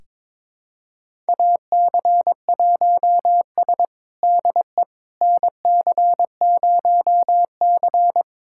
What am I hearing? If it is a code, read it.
Morse code, heard as a keyed tone, about 21 wpm, 700 Hz. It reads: AC1S DE NC0C
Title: USA Callsign DE Callsign - 4 Characters - 22wpm